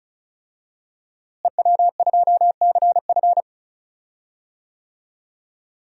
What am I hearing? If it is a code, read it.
Morse code, heard as a keyed tone, about 34 wpm, 700 Hz. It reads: EW2CF
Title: DXCC-100 Call Signs - 5 Characters - Repeated 3x - 35wpm